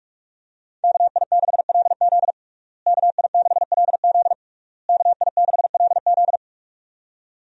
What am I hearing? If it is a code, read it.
Morse code, heard as a keyed tone, about 41 wpm, 700 Hz. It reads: KI6LZ KI6LZ KI6LZ
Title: USA Call Signs - 5 Characters - Repeated 3x - 45wpm